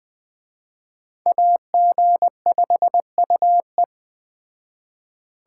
Morse code, heard as a keyed tone, about 20 wpm, 700 Hz. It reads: AG5UE
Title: Speed Racing - USA 2x2 Call Signs 17wpm